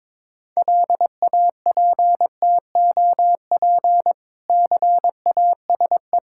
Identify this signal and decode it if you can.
Morse code, heard as a keyed tone, about 22 wpm, 690 Hz. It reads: LAPTOP CASE
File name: Difficult Words with Hints - 22wpm